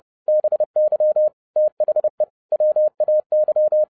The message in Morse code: BY THE WAY

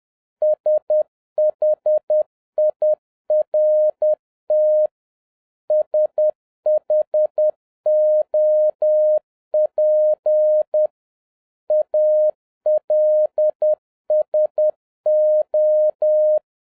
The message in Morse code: SHIRT SHOP ALSO

